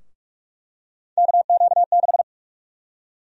Morse code, heard as a keyed tone, about 46 wpm, 700 Hz. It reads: KQ6
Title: USA Call Sign Prefixes - 45wpm